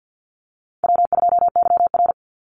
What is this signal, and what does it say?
Morse code, heard as a keyed tone, about 48 wpm, 700 Hz. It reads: W2YR
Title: Call Signs Repeated 2x - 50wpm